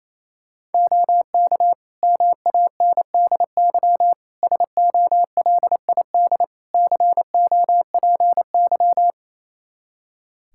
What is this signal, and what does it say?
Morse code, heard as a keyed tone, about 28 wpm, 700 Hz. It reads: OK MANDY SOLID COPY
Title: Practice QSO - Exchange 3 - More Info - 28wpm